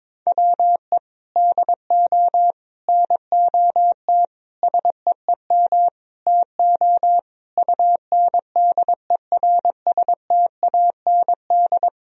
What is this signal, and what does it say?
Morse code, heard as a keyed tone, about 22 wpm, 700 Hz. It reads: WE DO NOT SEEM TO UNDERSTAND
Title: Sentences from Top 1000 Words - Review Part 1 - 22wpm